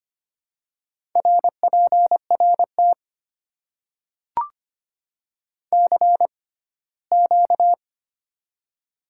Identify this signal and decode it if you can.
Morse code, heard as a keyed tone, about 25 wpm, spaced as for 17 wpm, 700 Hz. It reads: RPRT CQ
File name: Top 90 QSO Elements - 6x Character Spacing - 25wpm